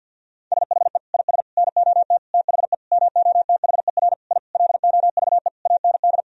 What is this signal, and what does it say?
Morse code, heard as a keyed tone, about 46 wpm, 700 Hz. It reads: SHE IS NOT THE MOTHER I LOVE AND
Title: Sentences from Top 1000 Words - Review Part 2 - 50wpm